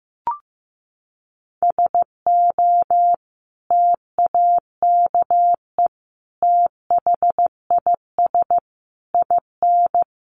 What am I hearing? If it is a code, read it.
Morse code, heard as a keyed tone, about 15 wpm, 700 Hz. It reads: SO TAKE THIS IN
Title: Sentences from Top 1500 Words 15wpm